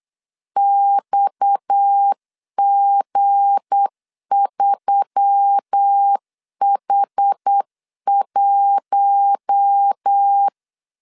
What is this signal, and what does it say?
Morse code, heard as a keyed tone, about 8 wpm, 800 Hz. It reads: XG3H1
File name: LEZIONI / 7 GF9Z / 7 G 8wpm 3118